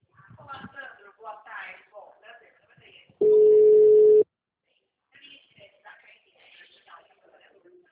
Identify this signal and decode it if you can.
Ringback tone (Europe, 425 Hz), with speech.